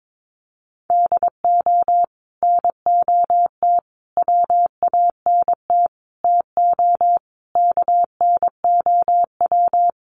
Morse code, heard as a keyed tone, about 22 wpm, 700 Hz. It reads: DO NOT WANT TO KNOW
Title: Sets of 5 Words Encore 22wpm